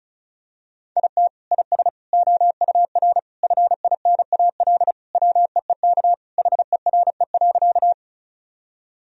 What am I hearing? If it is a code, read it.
Morse code, heard as a keyed tone, about 34 wpm, 700 Hz. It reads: IT IS OUR FINAL WEEK HERE.
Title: Sentences from Top 1000 Words - Review Part 2 - 35wpm